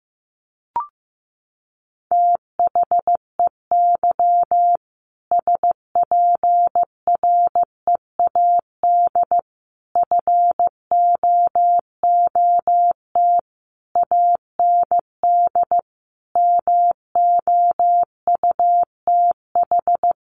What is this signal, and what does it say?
Morse code, heard as a keyed tone, about 14 wpm, 700 Hz. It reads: THEY SPREAD FOOT AND MOUTH
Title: Sentences from Top 1500 Words 15wpm